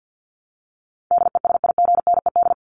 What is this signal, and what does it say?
Morse code, heard as a keyed tone, about 50 wpm, 700 Hz. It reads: DESIGNED
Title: ED Suffix - Top 100 Words with Base Words 50wpm